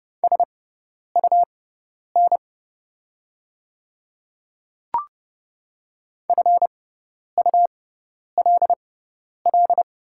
Morse code, heard as a keyed tone, about 31 wpm, spaced as for 10 wpm, 700 Hz. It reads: SUN FULL
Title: 6x Character Spacing - Top 500 Words - 30wpm